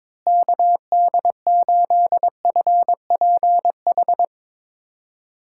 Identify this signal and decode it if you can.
Morse code, heard as a keyed tone, about 22 wpm, 700 Hz. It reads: KD8FPH
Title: USA Callsign DE Callsign 22wpm